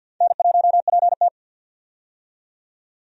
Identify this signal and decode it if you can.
Morse code, heard as a keyed tone, about 47 wpm, 700 Hz. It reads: N1PT
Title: USA Call Signs - 4 Characters - 50wpm